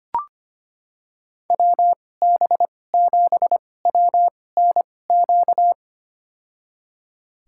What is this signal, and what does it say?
Morse code, heard as a keyed tone, about 25 wpm, spaced as for 18 wpm, 700 Hz. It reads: WB7WNQ
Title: USA Call Signs - 2x Character Spacing - 25wpm